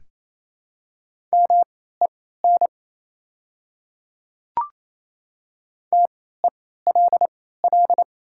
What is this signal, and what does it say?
Morse code, heard as a keyed tone, about 28 wpm, spaced as for 16 wpm, 690 Hz. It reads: MEN TELL
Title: 3x Character Spacing - Top 300 Words - 28wpm